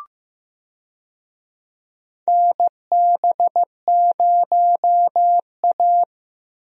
Morse code, heard as a keyed tone, about 15 wpm, 700 Hz. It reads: NB0A
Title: USA Call Signs - 4 Characters - Repeated 3x - 15wpm